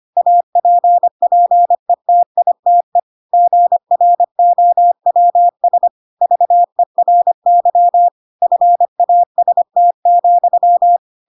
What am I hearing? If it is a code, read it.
Morse code, heard as a keyed tone, about 25 wpm, 700 Hz. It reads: APPETITE GROWS VERY FAST,
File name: Pinocchio / Pinocchio ch5 25 25 ik1hge